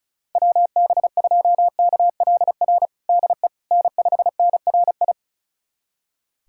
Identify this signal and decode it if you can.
Morse code, heard as a keyed tone, about 34 wpm, 700 Hz. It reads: WB2KLR DE N5NRI
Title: USA Callsign DE Callsign 35wpm